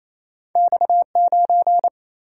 Morse code, heard as a keyed tone, about 28 wpm, 700 Hz. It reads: X9
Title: Single-Letter Single-Number 28wpm